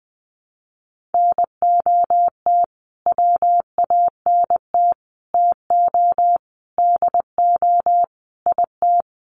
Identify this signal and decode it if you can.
Morse code, heard as a keyed tone, about 20 wpm, 700 Hz. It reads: NOT WANT TO DO IT
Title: Sets of 5 Words Encore 20wpm